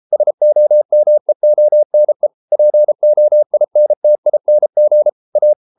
Morse code, heard as a keyed tone, about 33 wpm, 600 Hz. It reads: SOMEONE POINTING A